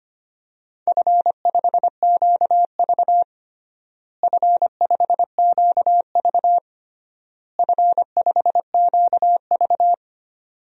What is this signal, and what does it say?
Morse code, heard as a keyed tone, about 25 wpm, 700 Hz. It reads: F5QV F5QV F5QV
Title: DXCC-100 Call Signs - 4 Characters - Repeated 3x - 25wpm